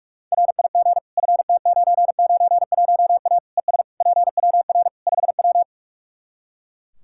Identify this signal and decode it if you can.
Morse code, heard as a keyed tone, about 46 wpm, 700 Hz. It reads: RIG FT991A ES PWR 5W
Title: Practice QSO - Exchange 2 - Info - 45wpm